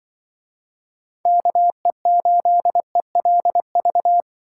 Morse code, heard as a keyed tone, about 24 wpm, 700 Hz. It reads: KE8ELV
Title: Speed Racing - USA 2x3 Call Signs 20wpm